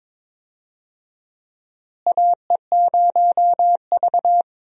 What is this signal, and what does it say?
Morse code, heard as a keyed tone, about 22 wpm, 700 Hz. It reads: AE0V
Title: USA Call Signs - 4 Characters - Repeated 3x - 22wpm